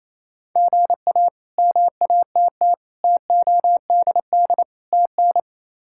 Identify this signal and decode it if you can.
Morse code, heard as a keyed tone, about 27 wpm, 700 Hz. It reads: GA MATT TODD TN
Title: K1USN SST QSO Practice - 28wpm